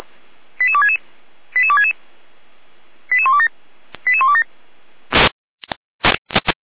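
Five-tone selective call (ZVEI), heard as quick runs of five tones, tones of about 70 ms. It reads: 88266 88266 88127 88127
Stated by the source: Received by SDR.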